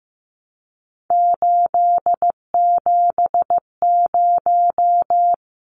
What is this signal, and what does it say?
Morse code, heard as a keyed tone, about 15 wpm, 700 Hz. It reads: 870
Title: Numbers - 3-Digit 15wpm